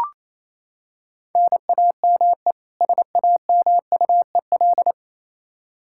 Morse code, heard as a keyed tone, about 27 wpm, 700 Hz. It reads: NAME SAMUEL
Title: Practice QSO - Exchange 1 - Beginning - 28wpm